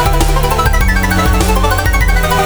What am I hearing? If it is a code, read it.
There is no code.